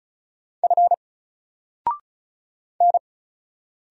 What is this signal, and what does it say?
Morse code, heard as a keyed tone, about 33 wpm, 700 Hz. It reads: F N
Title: Single Letter-Number Alt 35wpm